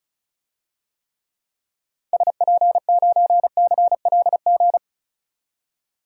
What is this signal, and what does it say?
Morse code, heard as a keyed tone, about 36 wpm, 700 Hz. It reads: SP9CLG
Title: DXCC-100 Call Signs - 6 Characters - 35wpm